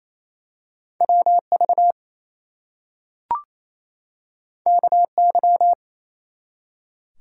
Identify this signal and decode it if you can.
Morse code, heard as a keyed tone, about 27 wpm, 700 Hz. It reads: WV KY
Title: US States and Canadian Provinces 28wpm